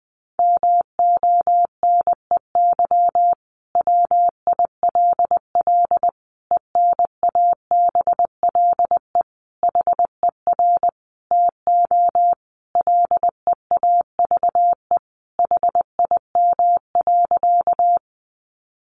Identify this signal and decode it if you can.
Morse code, heard as a keyed tone, about 20 wpm, 700 Hz. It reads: MONEY WILL ENABLE HER TO LEAVE HIM.
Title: Sentences from Top 1500 Words 20wpm